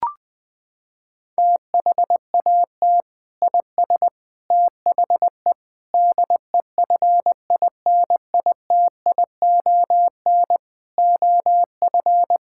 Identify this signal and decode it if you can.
Morse code, heard as a keyed tone, about 20 wpm, 700 Hz. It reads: THAT IS THE DEFINITION OF
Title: Sentences from Top 1000 Words - Review Part 6 - 20wpm